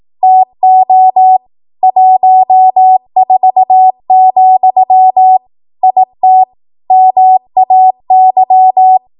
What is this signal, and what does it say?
Morse code, heard as a keyed tone, about 18 wpm, 750 Hz. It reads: TO 14, IT MAY
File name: Morse / 18WPM